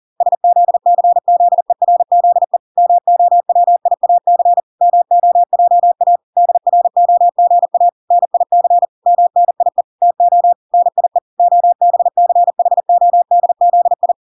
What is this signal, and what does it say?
Morse code, heard as a keyed tone, about 40 wpm, 700 Hz. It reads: SZCZERZE MOWIAC MOJA DROGA NIC MNIE TO NIE OBCHODZI